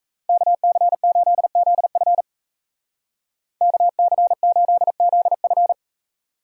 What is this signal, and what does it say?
Morse code, heard as a keyed tone, about 35 wpm, 700 Hz. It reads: KC8ZF KC8ZF
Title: Speed Racing - USA Call Signs 28wpm